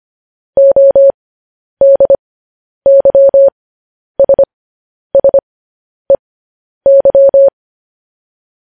Morse code, heard as a keyed tone, about 25 wpm, spaced as for 10 wpm, 550 Hz. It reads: ODYSSEY
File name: podcast / News headlines-1x-25x10